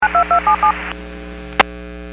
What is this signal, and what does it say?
DTMF, heard as telephone touch tones, tones of about 100 ms. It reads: 92277